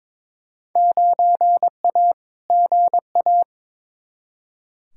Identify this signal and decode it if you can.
Morse code, heard as a keyed tone, about 22 wpm, 700 Hz. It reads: 9A GA
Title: Field Day Exchanges Encore - 22wpm